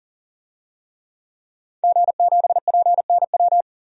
Morse code, heard as a keyed tone, about 40 wpm, 700 Hz. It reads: G7PNW
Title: DXCC-100 Call Signs - 5 Characters - 40wpm